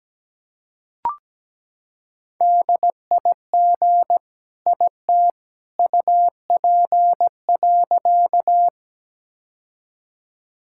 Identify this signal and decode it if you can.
Morse code, heard as a keyed tone, about 17 wpm, 700 Hz. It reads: DIG IT UP.
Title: Sentences from Top 1600 Words 17wpm